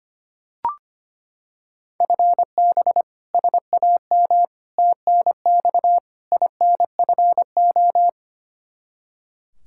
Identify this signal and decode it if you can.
Morse code, heard as a keyed tone, about 25 wpm, 700 Hz. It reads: FB SAM TNX INFO
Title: Practice QSO - Exchange 3 - More Info - 25wpm